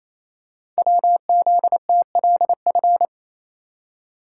Morse code, heard as a keyed tone, about 27 wpm, 700 Hz. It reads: WZTLF
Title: Code Groups - 5 Letters - 28wpm